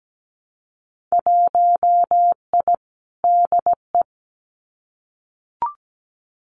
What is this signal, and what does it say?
Morse code, heard as a keyed tone, about 17 wpm, 700 Hz. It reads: 1I DE
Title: Winter Field Day Exchanges Encore 17wpm